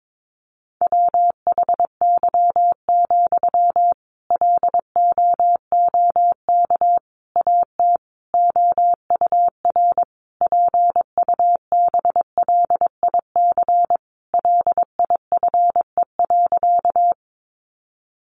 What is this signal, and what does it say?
Morse code, heard as a keyed tone, about 22 wpm, 700 Hz. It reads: WHY, LOOK AT OUR PUBLIC LIFE.
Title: Sentences from Top 600 Words 22wpm